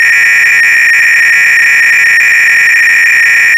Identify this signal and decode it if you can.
300-baud modem data.